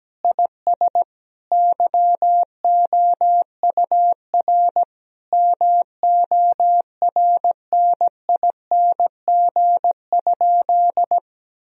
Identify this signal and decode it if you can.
Morse code, heard as a keyed tone, about 17 wpm, 700 Hz. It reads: IS YOUR MORNING?
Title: Sentences from Top 1000 Words - Review Part 3 - 17wpm